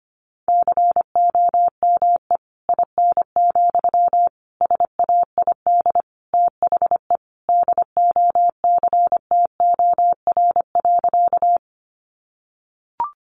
Morse code, heard as a keyed tone, about 25 wpm, 700 Hz. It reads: COME IN, SAID THE DOCTOR.
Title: Sentences from Top 1500 Words 25wpm